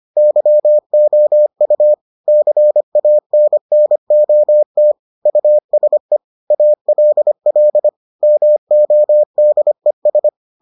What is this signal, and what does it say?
Morse code, heard as a keyed tone, about 25 wpm, 600 Hz. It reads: YOU CANNOT USE ALL MODES